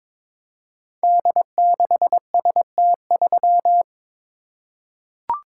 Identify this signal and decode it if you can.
Morse code, heard as a keyed tone, about 22 wpm, 700 Hz. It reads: D6ST3
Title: Code Groups - 5 Letter:Numbers - 22wpm